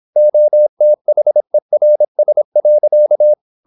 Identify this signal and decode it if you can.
Morse code, heard as a keyed tone, about 26 wpm, 600 Hz. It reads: OTHERS.
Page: Never 26wpm | CW med Gnister